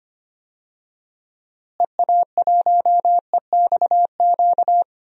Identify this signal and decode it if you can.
Morse code, heard as a keyed tone, about 25 wpm, 700 Hz. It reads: EA1EXQ
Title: DXCC-100 Call Signs - 6 Characters - Repeated 2x - 25wpm